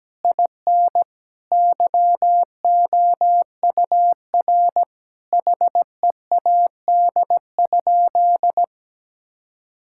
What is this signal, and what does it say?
Morse code, heard as a keyed tone, about 17 wpm, 700 Hz. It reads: IN YOUR HEAD?
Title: Sentences from Top 1000 Words - Review Part 5 - 17wpm